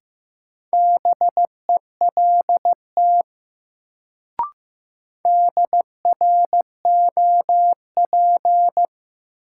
Morse code, heard as a keyed tone, about 15 wpm, 700 Hz. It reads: BELT DROP
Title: Sentences from Top 1400 Words - New Word Drill Alt - 15wpm